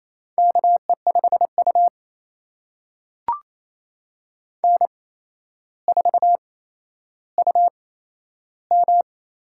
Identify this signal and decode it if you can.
Morse code, heard as a keyed tone, about 27 wpm, spaced as for 12 wpm, 700 Hz. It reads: KE5U N4UM
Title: USA Call Signs - 8x Character Spacing - 28wpm